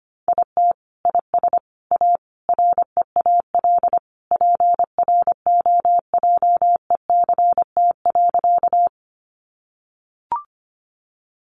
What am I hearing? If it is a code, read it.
Morse code, heard as a keyed tone, about 25 wpm, 700 Hz. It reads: IT IS A REAL PROJECT.
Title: Sentences from Top 1400 Words 25wpm